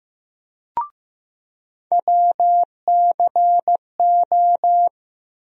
Morse code, heard as a keyed tone, about 15 wpm, 700 Hz. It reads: WCO
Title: USA Call Sign Postfixes - 3 Letters 15wpm